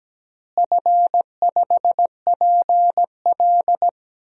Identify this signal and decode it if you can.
Morse code, heard as a keyed tone, about 17 wpm, 700 Hz. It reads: F5PL
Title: DXCC-100 Call Signs - 4 Characters - Repeated 2x - 17wpm